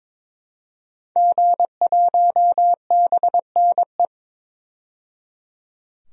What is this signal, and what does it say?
Morse code, heard as a keyed tone, about 21 wpm, 700 Hz. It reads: G1BNE
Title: DXCC-100 Call Signs - 5 Characters - 22wpm